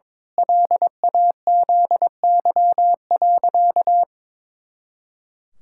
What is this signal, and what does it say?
Morse code, heard as a keyed tone, about 22 wpm, 700 Hz. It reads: LAZY.
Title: Pangram Challenge v2 - 22wpm